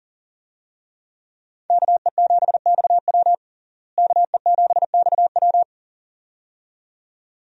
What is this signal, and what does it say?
Morse code, heard as a keyed tone, about 40 wpm, 700 Hz. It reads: KE7XW KE7XW
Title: USA Call Signs - 5 Characters - Repeated 2x - 40wpm